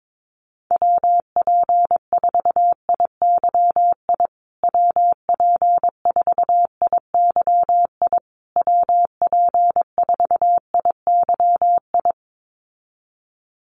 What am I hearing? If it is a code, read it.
Morse code, heard as a keyed tone, about 22 wpm, 700 Hz. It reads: WP4IYI WP4IYI WP4IYI
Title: USA Call Signs Repeated 3x 22wpm